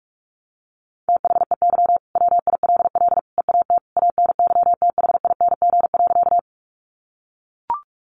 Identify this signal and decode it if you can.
Morse code, heard as a keyed tone, about 44 wpm, 700 Hz. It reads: THEY WILL EAT ANYTHING.
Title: Sentences from Top 1000 Words 45wpm